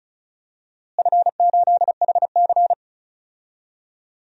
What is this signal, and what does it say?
Morse code, heard as a keyed tone, about 36 wpm, 700 Hz. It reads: F8HC
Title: DXCC-100 Call Signs - 4 Characters - Repeated 2x - 35wpm